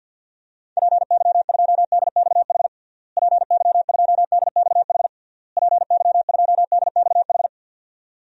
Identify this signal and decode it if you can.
Morse code, heard as a keyed tone, about 52 wpm, 700 Hz. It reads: PY2DXH PY2DXH PY2DXH
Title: DXCC-100 Call Signs - 6 Characters - Repeated 3x - 50wpm